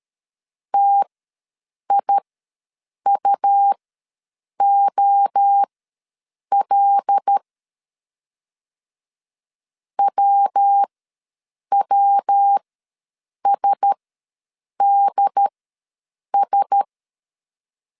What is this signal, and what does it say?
Morse code, heard as a keyed tone, about 13 wpm, spaced as for 7 wpm, 800 Hz. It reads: TIUOL WWSDS
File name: LEZIONI / 5 LBVC / 5 L ws 12wpm 2958